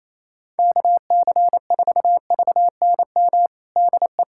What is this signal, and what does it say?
Morse code, heard as a keyed tone, about 28 wpm, 700 Hz. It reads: KC4VNM DE
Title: USA Callsign DE Callsign - 6 Characters - 28wpm